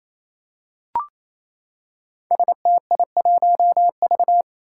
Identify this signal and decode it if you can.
Morse code, heard as a keyed tone, about 27 wpm, 700 Hz. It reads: STI1V
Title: Five Challenging Letter-Number Combinations 28wpm